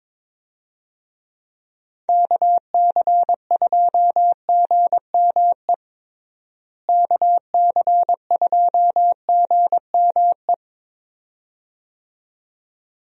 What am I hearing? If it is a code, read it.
Morse code, heard as a keyed tone, about 22 wpm, 700 Hz. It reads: KC2GME KC2GME
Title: USA Call Signs - 6 Characters - Repeated 2x - 22wpm